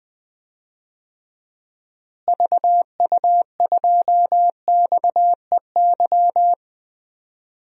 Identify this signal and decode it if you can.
Morse code, heard as a keyed tone, about 20 wpm, 700 Hz. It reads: VU2XEY
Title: DXCC-100 Call Signs - 6 Characters - Repeated 2x - 20wpm